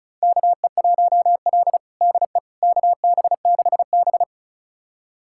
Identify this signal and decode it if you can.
Morse code, heard as a keyed tone, about 34 wpm, 700 Hz. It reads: KE1L DE KB6B
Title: USA Callsign DE Callsign - 4 Characters - 35wpm